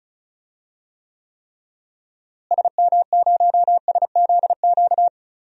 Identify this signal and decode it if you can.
Morse code, heard as a keyed tone, about 34 wpm, 700 Hz. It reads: SM0SZQ
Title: DXCC-100 Call Signs - 6 Characters - Repeated 3x - 35wpm